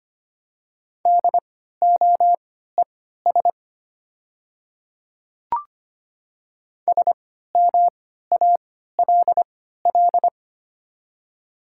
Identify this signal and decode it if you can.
Morse code, heard as a keyed tone, about 25 wpm, spaced as for 14 wpm, 700 Hz. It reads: DOES SMALL
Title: 3x Character Spacing - Top 300 Words - 25wpm